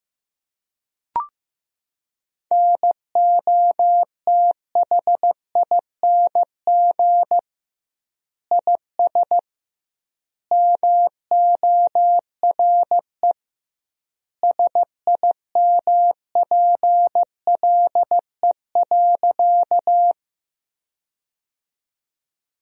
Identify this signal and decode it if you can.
Morse code, heard as a keyed tone, about 15 wpm, 700 Hz. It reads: NOTHING IS MORE SIMPLE.
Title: 2x Word-Spacing - Sentences from Top 500 Words - 15wpm